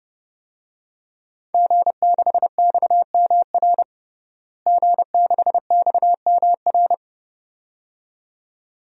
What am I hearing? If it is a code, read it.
Morse code, heard as a keyed tone, about 31 wpm, 700 Hz. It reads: G6XMR G6XMR
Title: DXCC-100 Call Signs - 5 Characters - Repeated 2x - 30wpm